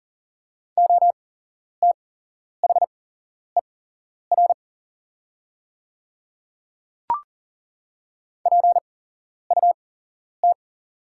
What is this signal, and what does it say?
Morse code, heard as a keyed tone, about 40 wpm, spaced as for 11 wpm, 700 Hz. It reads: OTHER PUT
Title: 8x Character Spacing - Top 300 Words - 40wpm